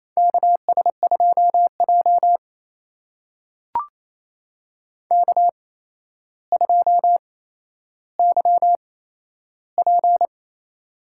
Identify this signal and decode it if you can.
Morse code, heard as a keyed tone, about 28 wpm, spaced as for 12 wpm, 700 Hz. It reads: KS2J K2YP
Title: USA Call Signs - 4 Characters - 8x Character Spacing - 28wpm